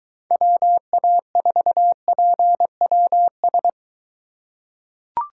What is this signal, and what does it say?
Morse code, heard as a keyed tone, about 21 wpm, 700 Hz. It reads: WA4PWS